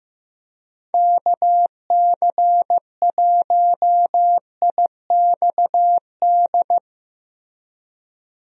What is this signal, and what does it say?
Morse code, heard as a keyed tone, about 15 wpm, 700 Hz. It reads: KC1IXD
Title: USA Call Signs - 6 Characters - Repeated 3x - 15wpm